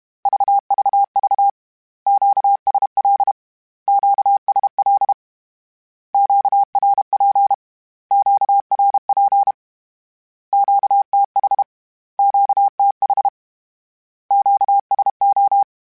Morse code, heard as a keyed tone, about 32 wpm, 800 Hz. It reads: VVV QSL QSL QRP QRP QTH QTH QSO
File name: Q Calls / Q Calls-mp3s / Q Calls Twice @30